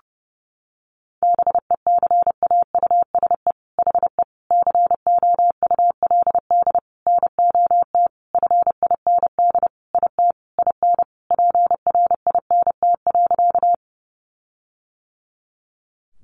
Morse code, heard as a keyed tone, about 30 wpm, 700 Hz. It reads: BECAUSE HE COULD NOT FIND IT IN PRINT.
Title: Sentences from Top 1400 Words 30wpm